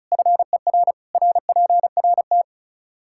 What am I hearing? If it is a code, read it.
Morse code, heard as a keyed tone, about 36 wpm, 700 Hz. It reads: FER RPRT
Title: Practice QSO - Exchange 2 - Info - 35wpm